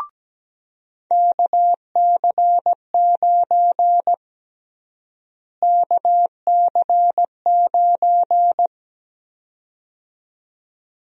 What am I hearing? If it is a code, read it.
Morse code, heard as a keyed tone, about 17 wpm, 700 Hz. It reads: KC9 KC9
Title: USA Call Sign Prefixes - Repeated 2x - 17wpm